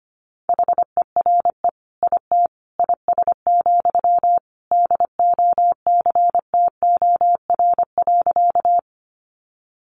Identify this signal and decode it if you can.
Morse code, heard as a keyed tone, about 25 wpm, 700 Hz. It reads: HERE IT IS, DOCTOR.
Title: Sentences from Top 1500 Words 25wpm